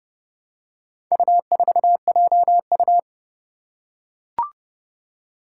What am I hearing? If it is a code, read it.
Morse code, heard as a keyed tone, about 31 wpm, 700 Hz. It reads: U4JU
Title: Four Challenging Letter-Number Combinations 30wpm